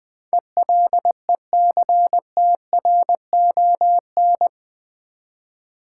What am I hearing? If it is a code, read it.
Morse code, heard as a keyed tone, about 20 wpm, 700 Hz. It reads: ELECTRON
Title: Build A Word - 4 Letters At A Time - 11-Letter Words - 20wpm